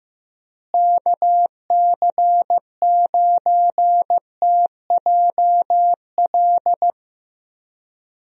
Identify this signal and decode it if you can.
Morse code, heard as a keyed tone, about 15 wpm, 700 Hz. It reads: KC9TJL
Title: USA Call Signs - 6 Characters - Repeated 3x - 15wpm